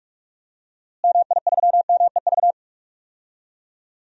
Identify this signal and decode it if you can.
Morse code, heard as a keyed tone, about 41 wpm, 700 Hz. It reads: MI3MEV
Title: DXCC-100 Call Signs - 6 Characters - Repeated 2x - 45wpm